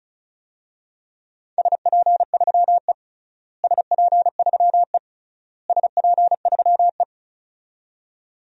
Morse code, heard as a keyed tone, about 34 wpm, 700 Hz. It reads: SP3E SP3E SP3E
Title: DXCC-100 Call Signs - 4 Characters - Repeated 3x - 35wpm